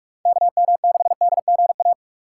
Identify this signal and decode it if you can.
Morse code, heard as a keyed tone, about 44 wpm, 700 Hz. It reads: KM6DGA
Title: USA Call Signs - 6 Characters - 6x Character Spacing - 45wpm